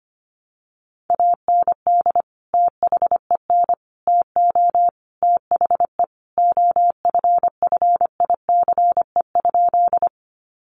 Morse code, heard as a keyed tone, about 25 wpm, 700 Hz. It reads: AND THEN TO THE OFFICE?
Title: Sentences from Top 1000 Words 25wpm